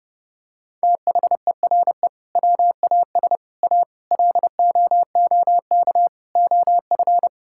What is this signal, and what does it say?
Morse code, heard as a keyed tone, about 30 wpm, 700 Hz. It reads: THERE WAS A LOOK OF
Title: Sentences from Top 1600 Words 30wpm